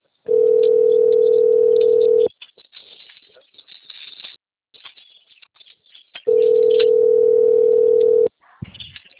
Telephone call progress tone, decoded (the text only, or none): ringback tone